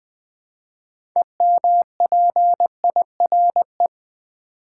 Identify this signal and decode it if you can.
Morse code, heard as a keyed tone, about 20 wpm, 700 Hz. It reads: EMPIRE
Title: Build A Word - 2 Letters At A Time - 6-Letter Words - 20wpm